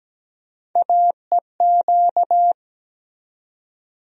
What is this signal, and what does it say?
Morse code, heard as a keyed tone, about 17 wpm, 700 Hz. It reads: AEQ
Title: USA Call Sign Postfixes - 3 Letters 17wpm